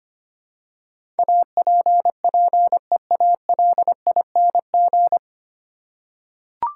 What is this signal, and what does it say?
Morse code, heard as a keyed tone, about 25 wpm, 700 Hz. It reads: APPEALING
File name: Nine Letter Words - 25wpm